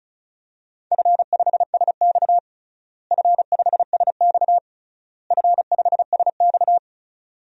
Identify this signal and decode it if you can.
Morse code, heard as a keyed tone, about 36 wpm, 700 Hz. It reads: F5SX F5SX F5SX
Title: DXCC-100 Call Signs - 4 Characters - Repeated 3x - 35wpm